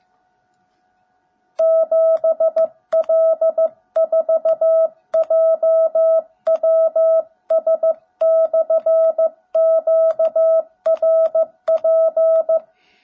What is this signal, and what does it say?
Morse code, heard as a keyed tone, about 14 wpm, 650 Hz. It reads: 7L4JWS/QRP
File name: jh8cbh / 2021cw / cw17b